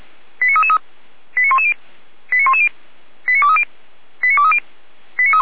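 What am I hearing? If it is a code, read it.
Five-tone selective call (ZVEI), heard as quick runs of five tones, tones of about 70 ms. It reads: 89333 89110 89110 89230 89230
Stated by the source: Five-tone ZVEI selective call received by SDR, fire brigade Orlamünde (feuerwehr-orlamuende).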